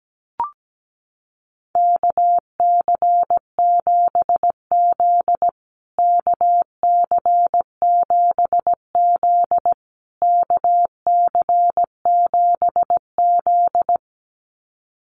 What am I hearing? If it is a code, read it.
Morse code, heard as a keyed tone, about 17 wpm, 700 Hz. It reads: KC7Z KC7Z KC7Z
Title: USA Call Signs Repeated 3x 17wpm